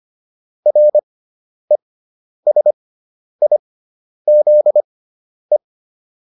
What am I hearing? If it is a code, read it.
Morse code, heard as a keyed tone, about 25 wpm, spaced as for 10 wpm, 600 Hz. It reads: RESIZE
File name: RE6 10